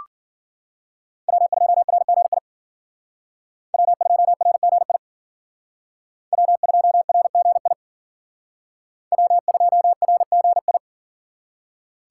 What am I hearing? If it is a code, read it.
Morse code, heard as a keyed tone, about 53 wpm, 700 Hz. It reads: W2RGI W2RGI W2RGI W2RGI